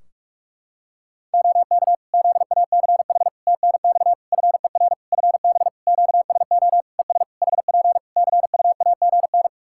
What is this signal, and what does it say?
Morse code, heard as a keyed tone, about 46 wpm, 700 Hz. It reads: OK ZACH TNX FER FB QSO ES HP CUAGN
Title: K1USN SST QSO Practice 45wpm